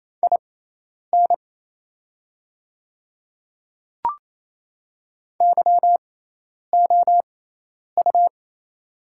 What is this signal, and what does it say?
Morse code, heard as a keyed tone, about 28 wpm, spaced as for 10 wpm, 700 Hz. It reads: IN YOU
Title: 6x Character Spacing - Top 100 Words - 28wpm